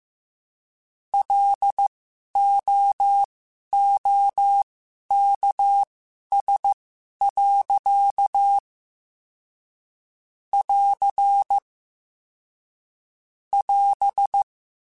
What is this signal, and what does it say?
Morse code, heard as a keyed tone, about 15 wpm, spaced as for 11 wpm, 800 Hz. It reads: LOOKS. <AR> <AS>
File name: Quotes / Quotes-mp3s / A Smile @13